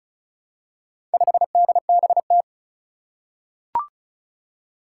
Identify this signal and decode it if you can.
Morse code, heard as a keyed tone, about 36 wpm, 700 Hz. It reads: 5DBT
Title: Four Challenging Letter-Number Combinations 35wpm